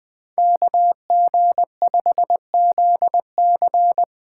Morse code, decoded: KG5ZC

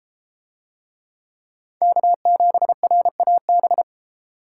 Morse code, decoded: K7RAB

